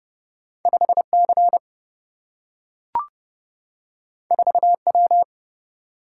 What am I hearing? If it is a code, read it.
Morse code, heard as a keyed tone, about 29 wpm, 700 Hz. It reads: 5C 4W
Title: Single-Letter Single-Number 30wpm